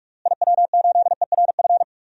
Morse code, heard as a keyed tone, about 46 wpm, 700 Hz. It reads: IW8ERF
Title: DXCC-100 Call Signs - 6 Characters - Repeated 3x - 45wpm